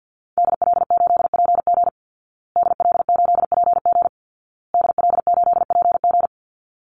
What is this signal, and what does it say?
Morse code, heard as a keyed tone, about 50 wpm, 700 Hz. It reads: DL8PG DL8PG DL8PG
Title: Call Signs Repeated 2x - 50wpm